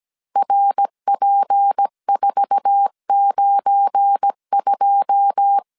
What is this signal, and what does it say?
Morse code, heard as a keyed tone, about 17 wpm, 800 Hz. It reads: RP492
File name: CW / DET / 1 9 1500C S A 16wpm 2828